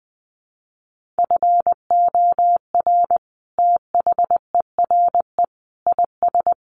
Morse code, heard as a keyed tone, about 20 wpm, 700 Hz. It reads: FOR THERE IS
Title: Sentences from Top 1400 Words 20wpm